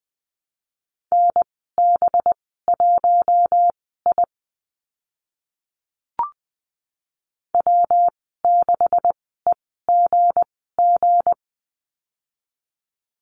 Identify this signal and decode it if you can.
Morse code, heard as a keyed tone, about 20 wpm, spaced as for 15 wpm, 700 Hz. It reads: NB1I W6EGG
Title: 193 Lesson 31g - Letter v - Call Signs 20wpm